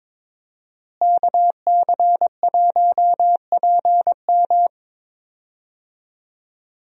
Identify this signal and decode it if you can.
Morse code, heard as a keyed tone, about 22 wpm, 700 Hz. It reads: KC1PM
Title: USA Call Signs - 5 Characters - Repeated 2x - 22wpm